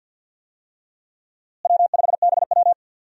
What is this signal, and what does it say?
Morse code, heard as a keyed tone, about 53 wpm, 700 Hz. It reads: W5BW
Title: USA Call Signs - 4 Characters - Repeated 2x - 50wpm